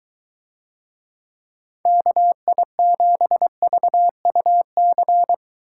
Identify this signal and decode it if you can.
Morse code, heard as a keyed tone, about 23 wpm, 700 Hz. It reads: KI7VUC